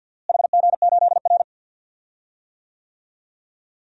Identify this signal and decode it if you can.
Morse code, heard as a keyed tone, about 48 wpm, 700 Hz. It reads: HG8R
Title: DXCC-100 Call Signs - 4 Characters - Repeated 2x - 50wpm